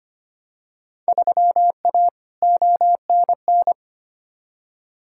Morse code, heard as a keyed tone, about 25 wpm, 700 Hz. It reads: 3A ONN
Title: Field Day Exchanges Encore - 25wpm